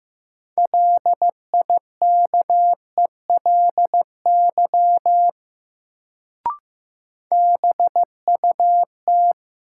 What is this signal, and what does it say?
Morse code, heard as a keyed tone, about 15 wpm, 700 Hz. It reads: LIKELY BUT
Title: Sets of 2 Words - New Word Drill - 15wpm